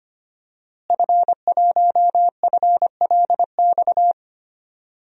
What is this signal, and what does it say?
Morse code, heard as a keyed tone, about 25 wpm, 700 Hz. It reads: F1FLX
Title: DXCC-100 Call Signs - 5 Characters - Repeated 3x - 25wpm